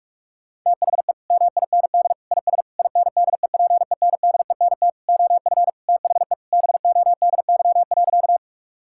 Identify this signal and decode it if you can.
Morse code, heard as a keyed tone, about 46 wpm, 700 Hz. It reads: THE MIND IS INDEPENDENT OF THE BODY.
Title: Sentences from Top 1000 Words - Review Part 6 - 45wpm